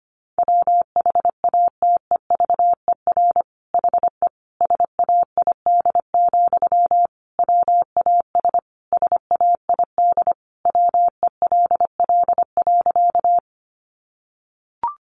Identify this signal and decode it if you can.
Morse code, heard as a keyed tone, about 25 wpm, 700 Hz. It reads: WHATEVER HE SAID, WAS SAID WELL.
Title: Sentences from Top 1100 Words 25wpm